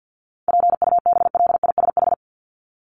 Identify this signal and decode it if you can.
Morse code, heard as a keyed tone, about 49 wpm, 700 Hz. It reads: PUBLISH
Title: ED Suffix - Top 100 Words with Base Words 50wpm